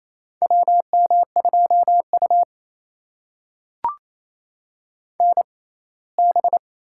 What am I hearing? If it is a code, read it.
Morse code, heard as a keyed tone, about 28 wpm, spaced as for 19 wpm, 700 Hz. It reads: WM2U NB
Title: USA Call Signs - 4 Characters - 6x Character Spacing - 28wpm